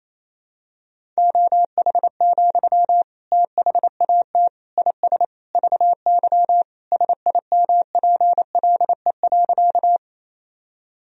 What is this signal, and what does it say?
Morse code, heard as a keyed tone, about 28 wpm, 700 Hz. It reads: OH, THAT IS VY SIMPLE.
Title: Sentences from Top 500 Words and Abbreviations 28wpm